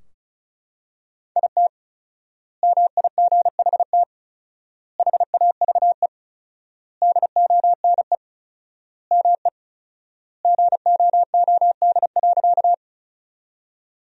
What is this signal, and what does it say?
Morse code, heard as a keyed tone, about 36 wpm, 700 Hz. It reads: IT MIGHT HAVE DONE ME GOOD.
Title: 4x Word-Spacing - Sentences from Top 500 Words - 35wpm